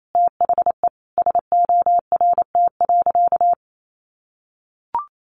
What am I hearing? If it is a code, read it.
Morse code, heard as a keyed tone, about 28 wpm, 700 Hz. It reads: THE SORT.
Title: Sentences from Top 1300 Words 28wpm